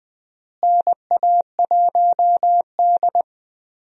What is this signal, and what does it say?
Morse code, heard as a keyed tone, about 20 wpm, 700 Hz. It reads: NA1D
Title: USA Call Signs - 4 Characters - Repeated 3x - 20wpm